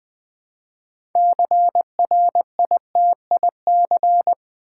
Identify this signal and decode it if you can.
Morse code, heard as a keyed tone, about 20 wpm, 700 Hz. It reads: CRITIC
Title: Build A Word - 3 Letters At A Time - 9-Letter Words - 20wpm